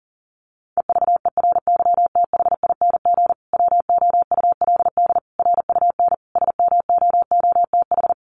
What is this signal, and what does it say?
Morse code, heard as a keyed tone, about 37 wpm, 700 Hz. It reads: EVERYTHING WOULD RUN SMOOTH
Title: Sentences from Top 1400 Words 40wpm